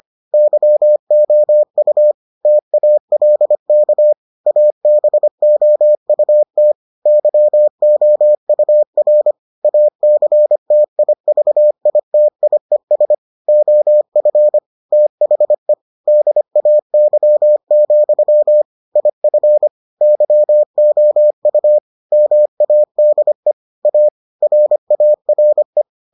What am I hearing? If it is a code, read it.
Morse code, heard as a keyed tone, about 25 wpm, 600 Hz. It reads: YOU TALK ABOUT YOUR ACTIVITIES OF THE DAY, IF YOU MADE A RARE